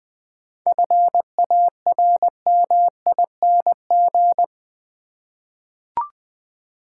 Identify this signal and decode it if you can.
Morse code, heard as a keyed tone, about 20 wpm, 700 Hz. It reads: FARMING
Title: 7 Letter Words - First 3 Letters Given - 20wpm